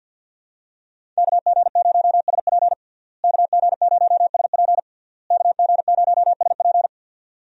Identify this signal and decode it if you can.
Morse code, heard as a keyed tone, about 49 wpm, 700 Hz. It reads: KG0SP KG0SP KG0SP
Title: USA Call Signs - 5 Characters - Repeated 3x - 50wpm